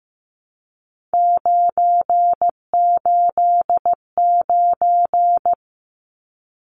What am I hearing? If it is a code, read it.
Morse code, heard as a keyed tone, about 15 wpm, 700 Hz. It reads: 989
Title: Numbers - 3-Digit 15wpm